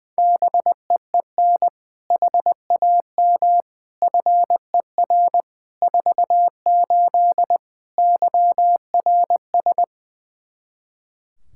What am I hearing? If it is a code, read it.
Morse code, heard as a keyed tone, about 21 wpm, 700 Hz. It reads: BEEN HAM FER 48 YRS
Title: Practice QSO - Exchange 3 - More Info - 20wpm